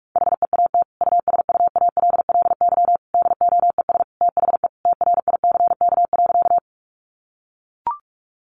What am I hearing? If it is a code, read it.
Morse code, heard as a keyed tone, about 44 wpm, 700 Hz. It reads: HEAT USUALLY DOES THE TRICK.
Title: Sentences from Top 1500 Words 45wpm